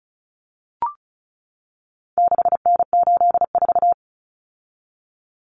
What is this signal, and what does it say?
Morse code, heard as a keyed tone, about 34 wpm, 700 Hz. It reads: 6N84
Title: Common Cut Numbers - 4-Digit 35wpm